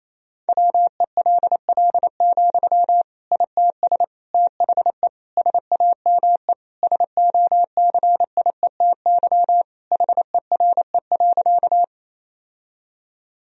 Morse code, heard as a keyed tone, about 27 wpm, 700 Hz. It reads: WELL, ITS THE SAME SOCIETY HERE.
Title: Sentences from Top 1000 Words - Review Part 4 - 28wpm